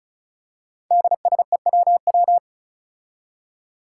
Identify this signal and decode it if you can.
Morse code, heard as a keyed tone, about 36 wpm, 700 Hz. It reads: DSEWW
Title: Five Challenging Letter-Number Combinations 35wpm